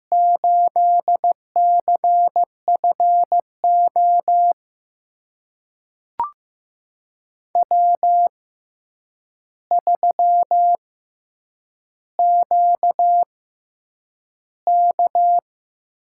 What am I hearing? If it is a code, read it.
Morse code, heard as a keyed tone, about 15 wpm, spaced as for 8 wpm, 700 Hz. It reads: 8CFO W3QK
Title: USA Call Signs - 5 Characters - 6x Character Spacing - 15wpm